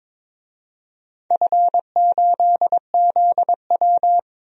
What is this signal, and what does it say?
Morse code, heard as a keyed tone, about 21 wpm, 700 Hz. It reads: F8ZW
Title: DXCC-100 Call Signs - 4 Characters - Repeated 3x - 22wpm